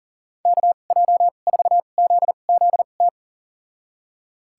Morse code, heard as a keyed tone, about 40 wpm, spaced as for 29 wpm, 700 Hz. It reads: KJ4ZZT
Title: USA Call Signs - 2x Character Spacing - 40wpm